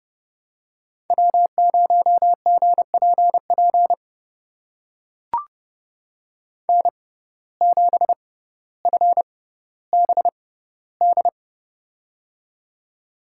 Morse code, heard as a keyed tone, about 32 wpm, spaced as for 16 wpm, 700 Hz. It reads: W0GPP N7FBD
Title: USA Call Signs - 5 Characters - 6x Character Spacing - 30wpm